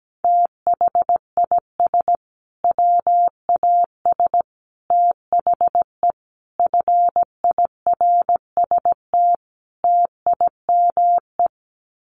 Morse code, heard as a keyed tone, about 17 wpm, 700 Hz. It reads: THIS WAS THE FIRST TIME
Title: Sets of 5 Words Encore 17wpm